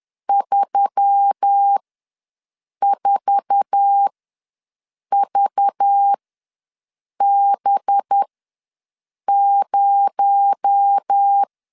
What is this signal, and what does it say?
Morse code, heard as a keyed tone, about 10 wpm, spaced as for 6 wpm, 800 Hz. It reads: 34VB0
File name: LEZIONI / 8 YJ45 / 8 YJ4 ws 10wpm 4020